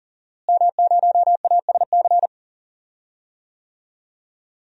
Morse code, heard as a keyed tone, about 40 wpm, 700 Hz. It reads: M0ASC
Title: DXCC-100 Call Signs - 5 Characters - Repeated 2x - 40wpm